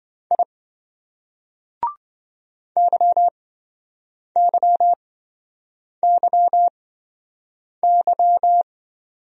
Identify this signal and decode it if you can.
Morse code, heard as a keyed tone, about 24 wpm, spaced as for 7 wpm, 700 Hz. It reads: I YYYY
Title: Speed-Racing Single Letters 20wpm